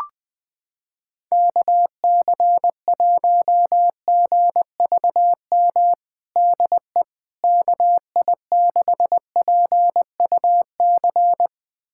Morse code, heard as a keyed tone, about 20 wpm, 700 Hz. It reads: KC1GVM DE KI6PUC
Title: USA Callsign DE Callsign 20wpm